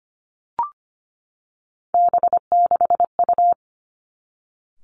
Morse code, heard as a keyed tone, about 25 wpm, 700 Hz. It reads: B6U
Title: Three Challenging Letter-Number Combinations 25wpm